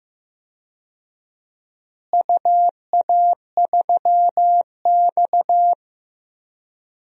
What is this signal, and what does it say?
Morse code, heard as a keyed tone, about 15 wpm, 700 Hz. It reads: UA3X
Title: DXCC-100 Call Signs - 4 Characters - Repeated 2x - 15wpm